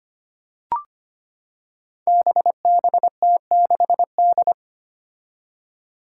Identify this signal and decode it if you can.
Morse code, heard as a keyed tone, about 25 wpm, 700 Hz. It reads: BBT6D
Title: Five Challenging Letter-Number Combinations 25wpm